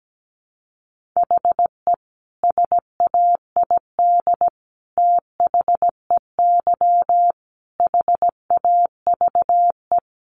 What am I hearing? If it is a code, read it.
Morse code, heard as a keyed tone, about 17 wpm, 700 Hz. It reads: HE SAID THEY HAVE
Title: Sentences from Top 800 Words 17wpm